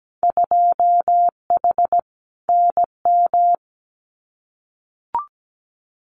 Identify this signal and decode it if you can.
Morse code, heard as a keyed tone, about 17 wpm, 700 Hz. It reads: 2H NM
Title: Winter Field Day Exchanges Encore 17wpm